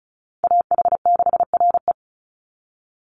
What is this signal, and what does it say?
Morse code, heard as a keyed tone, about 34 wpm, 700 Hz. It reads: AH6RE
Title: Call Signs Repeated 3x - 35wpm